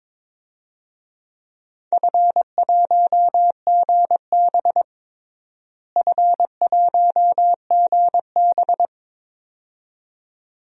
Morse code, heard as a keyed tone, about 22 wpm, 700 Hz. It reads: F1GB F1GB
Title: DXCC-100 Call Signs - 4 Characters - Repeated 2x - 22wpm